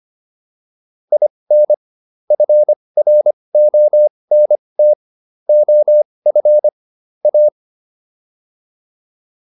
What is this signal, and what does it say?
Morse code, heard as a keyed tone, about 25 wpm, spaced as for 20 wpm, 600 Hz. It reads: IN FRONT OF A